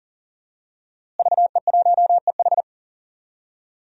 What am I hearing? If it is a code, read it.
Morse code, heard as a keyed tone, about 40 wpm, 700 Hz. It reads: VE1EH
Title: DXCC-100 Call Signs - 5 Characters - Repeated 2x - 40wpm